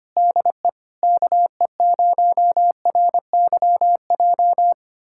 Morse code, heard as a keyed tone, about 25 wpm, 700 Hz. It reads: DE KE0RYJ
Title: USA Callsign DE Callsign - 6 Characters - 25wpm